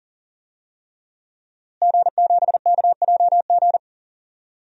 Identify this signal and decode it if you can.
Morse code, heard as a keyed tone, about 40 wpm, 700 Hz. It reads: G7KJG
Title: DXCC-100 Call Signs - 5 Characters - Repeated 3x - 40wpm